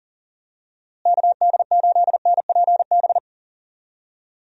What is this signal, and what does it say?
Morse code, heard as a keyed tone, about 39 wpm, 700 Hz. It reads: KD8NPB
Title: USA Call Signs - 6 Characters - Repeated 3x - 40wpm